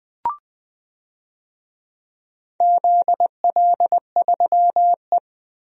Morse code, heard as a keyed tone, about 21 wpm, 700 Hz. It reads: ZL3E
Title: DXCC-100 Call Signs - 4 Characters - Repeated 3x - 20wpm